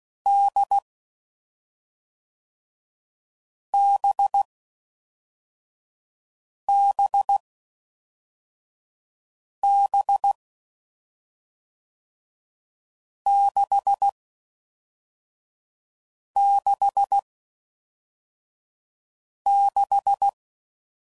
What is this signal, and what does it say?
Morse code, heard as a keyed tone, about 16 wpm, 800 Hz. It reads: D B B B 6 6 6
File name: Combinations / Difficult Combos / NDB NDB6